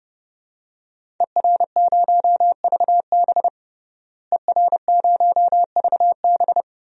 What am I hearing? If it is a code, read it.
Morse code, heard as a keyed tone, about 30 wpm, 700 Hz. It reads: ER0VB ER0VB
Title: DXCC-100 Call Signs - 5 Characters - Repeated 3x - 30wpm